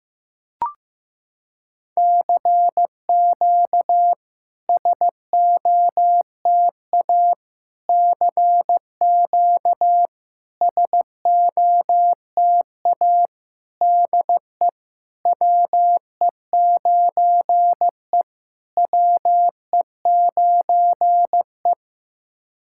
Morse code, heard as a keyed tone, about 15 wpm, 700 Hz. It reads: CQ SOTA CQ SOTA DE WE9E WE9E
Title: SOTA QSO Practice 15wpm